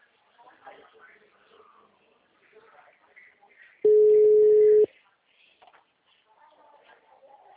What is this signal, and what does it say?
Ringback tone (Europe, 425 Hz), with speech.
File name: out- -2009- - - .18385